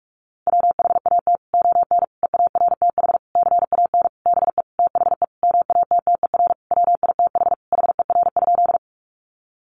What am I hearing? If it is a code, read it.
Morse code, heard as a keyed tone, about 44 wpm, 700 Hz. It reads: WHAT ON EARTH CAN BE THE MATTER WITH HER?
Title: Sentences from Top 700 Words 45wpm